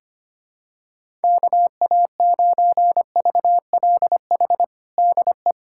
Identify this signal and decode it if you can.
Morse code, heard as a keyed tone, about 25 wpm, 700 Hz. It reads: KA9VLH DE
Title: USA Callsign DE Callsign - 6 Characters - 25wpm